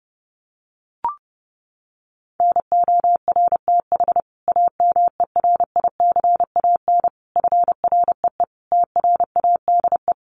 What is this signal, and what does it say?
Morse code, heard as a keyed tone, about 28 wpm, 700 Hz. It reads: NORTH AMERICAN FREE TRADE
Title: Sets of 5 Words Encore 30wpm